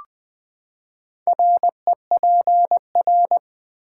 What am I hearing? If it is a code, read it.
Morse code, heard as a keyed tone, about 20 wpm, 700 Hz. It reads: REPR
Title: Build A Word - 4 Letters At A Time - 12-Letter Words - 20wpm